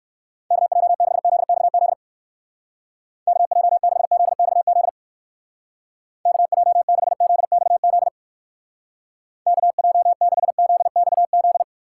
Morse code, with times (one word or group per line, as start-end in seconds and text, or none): 0.5-1.9 KJ6ZXZ
3.3-4.9 KJ6ZXZ
6.3-8.1 KJ6ZXZ
9.5-11.6 KJ6ZXZ